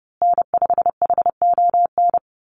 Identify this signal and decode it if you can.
Morse code, heard as a keyed tone, about 30 wpm, 700 Hz. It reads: N5HON
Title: USA Call Signs Repeated 2x 30wpm